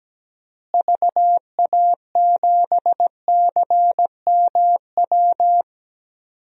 Morse code, heard as a keyed tone, about 17 wpm, 700 Hz. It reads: VA7CMW
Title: DXCC-100 Call Signs - 6 Characters - Repeated 3x - 17wpm